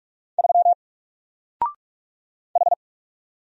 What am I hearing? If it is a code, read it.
Morse code, heard as a keyed tone, about 44 wpm, 700 Hz. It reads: 3 H
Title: Single Character (Letter:Number:Punctuation:Prosign) 45wpm